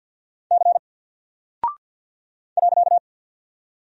Morse code, heard as a keyed tone, about 48 wpm, 700 Hz. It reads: C .
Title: Single Character (Letter:Number:Punctuation:Prosign) Alt 50wpm